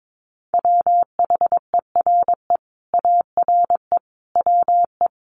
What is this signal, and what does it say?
Morse code, heard as a keyed tone, about 22 wpm, 700 Hz. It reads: WHERE ARE WE
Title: Sentences from Top 200 Words 22wpm